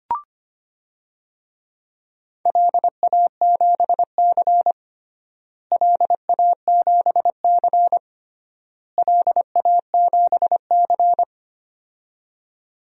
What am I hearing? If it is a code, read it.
Morse code, heard as a keyed tone, about 25 wpm, 700 Hz. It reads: LA7C LA7C LA7C
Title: DXCC-100 Call Signs - 4 Characters - Repeated 3x - 25wpm